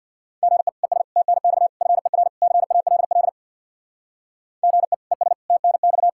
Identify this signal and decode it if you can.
Morse code, heard as a keyed tone, about 62 wpm, 700 Hz. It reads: GE ES TNX FER CALL GE ES TNX
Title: Speed-Racing Common QSO Phrases - 50wpm